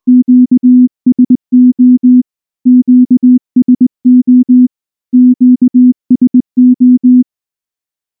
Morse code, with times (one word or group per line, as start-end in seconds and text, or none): 0.1-2.2 QSO
2.7-4.7 QSO
5.1-7.2 QSO